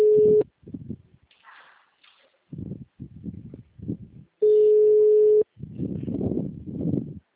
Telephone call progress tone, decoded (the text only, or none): ringback tone